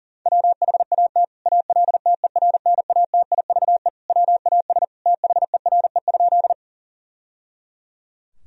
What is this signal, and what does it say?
Morse code, heard as a keyed tone, about 40 wpm, 700 Hz. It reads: WHAT ALTERNATIVE WAS THERE?
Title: Sentences from Top 1000 Words - Review Part 4 - 40wpm